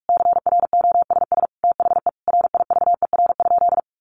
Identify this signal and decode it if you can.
Morse code, heard as a keyed tone, about 44 wpm, 700 Hz. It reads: CROSS THE RIVER?
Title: Sentences from Top 1400 Words 45wpm